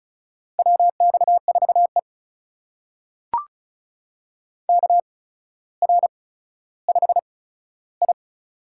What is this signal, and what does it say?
Morse code, heard as a keyed tone, about 36 wpm, spaced as for 15 wpm, 700 Hz. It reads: WX4E KR5I
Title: USA Call Signs - 4 Characters - 8x Character Spacing - 35wpm